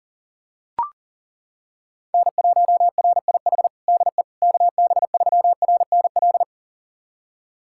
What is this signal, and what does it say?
Morse code, heard as a keyed tone, about 37 wpm, 700 Hz. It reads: N1RIH DE KB3RNL
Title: USA Callsign DE Callsign 40wpm